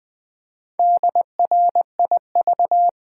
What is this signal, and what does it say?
Morse code, heard as a keyed tone, about 20 wpm, 700 Hz. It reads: DRIV
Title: Build A Word - 1 Letter At A Time - 5-Letter Words - 20wpm